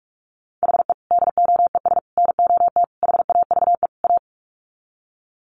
Morse code, heard as a keyed tone, about 44 wpm, 700 Hz. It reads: HE DOES NOT HAVE A
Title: Sets of 5 Words Encore 45wpm